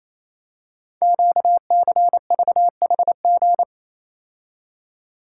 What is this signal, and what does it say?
Morse code, heard as a keyed tone, about 29 wpm, 700 Hz. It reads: QCVHG
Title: Code Groups - 5 Letters - 28wpm